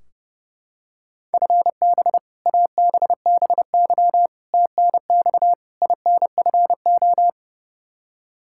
Morse code, heard as a keyed tone, about 29 wpm, 700 Hz. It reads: FB ABBY TNX INFO
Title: Practice QSO - Exchange 3 - More Info - 30wpm